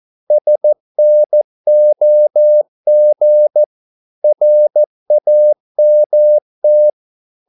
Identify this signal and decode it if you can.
Morse code, heard as a keyed tone, about 14 wpm, 600 Hz. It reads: SNOG RAMT
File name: Korte ord DK 14wpm